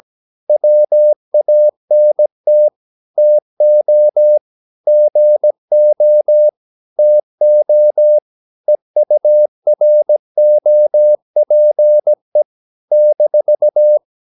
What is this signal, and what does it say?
Morse code, heard as a keyed tone, about 17 wpm, 600 Hz. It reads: WANT TO GO TO EUROPE -
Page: Romanen Never af Ken Follet på engelsk.